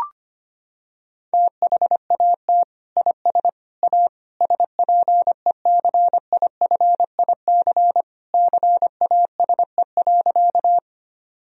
Morse code, heard as a keyed tone, about 25 wpm, 700 Hz. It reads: THAT IS A SPECIFIC CASE.
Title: Sentences from Top 1000 Words - Review Part 3 - 25wpm